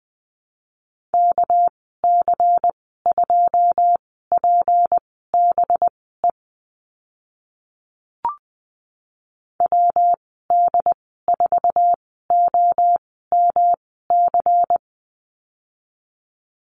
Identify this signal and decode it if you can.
Morse code, heard as a keyed tone, about 20 wpm, spaced as for 14 wpm, 700 Hz. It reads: KC2PBE WD4OMC
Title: 219 Lesson 35g - Number 0 - Call Signs 20wpm